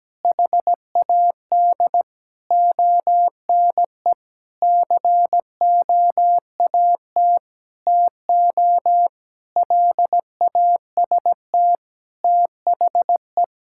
Morse code, heard as a keyed tone, about 17 wpm, 700 Hz. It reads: HAD ONE COAT TO LAST THE
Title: Sentences from Top 1600 Words 17wpm